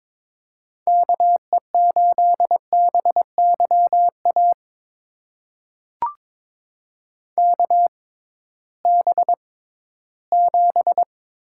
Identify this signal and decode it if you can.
Morse code, heard as a keyed tone, about 22 wpm, spaced as for 14 wpm, 700 Hz. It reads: KE8BYA KB7
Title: USA Call Signs - 6 Characters - 6x Character Spacing - 22wpm